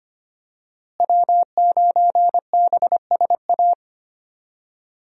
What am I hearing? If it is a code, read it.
Morse code, heard as a keyed tone, about 25 wpm, 700 Hz. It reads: W9BSA
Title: USA Call Signs - 5 Characters - Repeated 2x - 25wpm